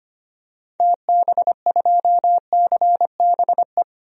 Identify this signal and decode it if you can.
Morse code, heard as a keyed tone, about 25 wpm, 700 Hz. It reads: TB2CBE
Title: DXCC-100 Call Signs - 6 Characters - 25wpm